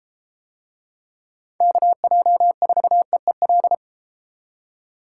Morse code, heard as a keyed tone, about 31 wpm, 700 Hz. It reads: KJ4EEL